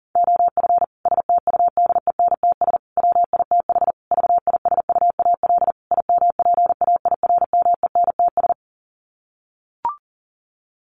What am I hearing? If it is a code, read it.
Morse code, heard as a keyed tone, about 37 wpm, 700 Hz. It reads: OF STUDENTS WITH VISUAL IMPAIRMENTS
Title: Sets of 5 Words Encore 40wpm